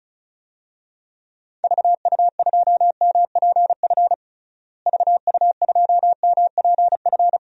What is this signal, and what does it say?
Morse code, heard as a keyed tone, about 34 wpm, 700 Hz. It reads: VU2MPF VU2MPF
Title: DXCC-100 Call Signs - 6 Characters - Repeated 2x - 35wpm